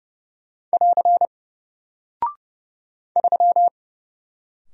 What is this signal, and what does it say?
Morse code, heard as a keyed tone, about 30 wpm, 700 Hz. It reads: <AR> 3
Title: Single Character (Letter:Number:Punctuation:Prosign) Alt 30wpm